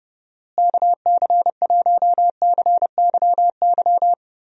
Morse code, heard as a keyed tone, about 30 wpm, 700 Hz. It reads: KC1CYY
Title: USA Call Signs - 6 Characters - Repeated 2x - 30wpm